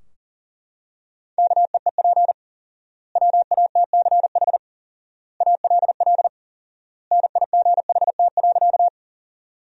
Morse code, heard as a keyed tone, about 40 wpm, 700 Hz. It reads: KEEP WATCH ALL NIGHT.
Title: 4x Word-Spacing - Sentences from Top 300 Words - 40wpm